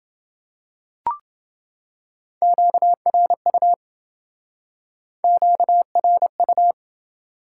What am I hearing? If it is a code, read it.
Morse code, heard as a keyed tone, about 28 wpm, 700 Hz. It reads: QRU QRU